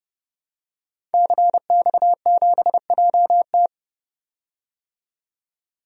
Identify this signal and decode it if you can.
Morse code, heard as a keyed tone, about 29 wpm, 700 Hz. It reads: CX7JT
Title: DXCC-100 Call Signs - 5 Characters - 30wpm